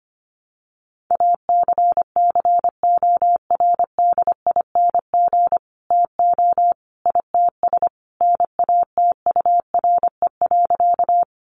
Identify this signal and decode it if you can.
Morse code, heard as a keyed tone, about 25 wpm, 700 Hz. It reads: ACCORDING TO ITS NATURE.
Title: Sentences from Top 700 Words 25wpm